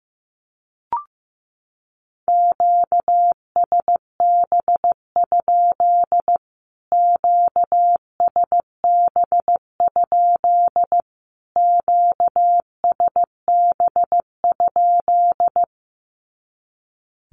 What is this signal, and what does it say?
Morse code, heard as a keyed tone, about 15 wpm, 700 Hz. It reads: QSB? QSB? QSB?
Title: Q-Codes Repeated 3x 15wpm